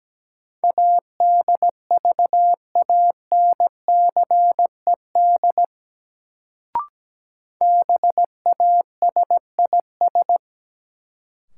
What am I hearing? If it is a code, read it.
Morse code, heard as a keyed tone, about 17 wpm, 700 Hz. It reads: ADVANCED BASIS
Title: Sentences from Top 1200 Words - New Word Drill Alt - 17wpm